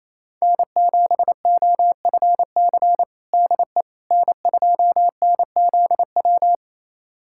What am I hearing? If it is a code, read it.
Morse code, heard as a keyed tone, about 28 wpm, 700 Hz. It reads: N7OFC DE N2NZW
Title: USA Callsign DE Callsign - 5 Characters - 28wpm